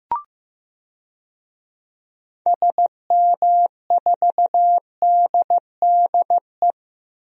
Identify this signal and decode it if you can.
Morse code, heard as a keyed tone, about 15 wpm, 700 Hz. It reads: SM4DDE
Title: DXCC-100 Call Signs - 6 Characters - Repeated 2x - 15wpm